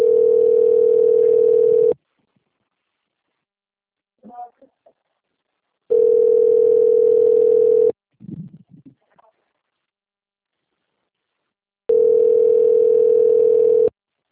Ringback tone (North America), 440 + 480 Hz.